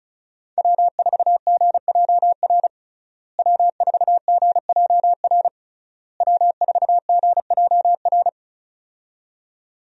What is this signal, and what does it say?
Morse code, heard as a keyed tone, about 36 wpm, 700 Hz. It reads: W4GJR W4GJR W4GJR
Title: USA Call Signs - 5 Characters - Repeated 3x - 35wpm